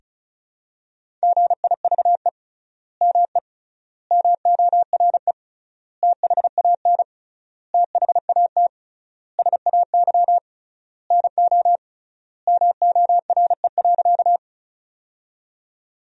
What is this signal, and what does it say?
Morse code, heard as a keyed tone, about 36 wpm, 700 Hz. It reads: GIVE ME MORE THAN THAT SAY NO MORE.
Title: 3x Word-Spacing - Sentences from Top 200 Words - 35wpm